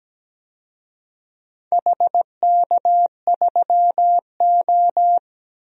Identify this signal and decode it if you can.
Morse code, heard as a keyed tone, about 17 wpm, 700 Hz. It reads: HK3O
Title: DXCC-100 Call Signs - 4 Characters - Repeated 2x - 17wpm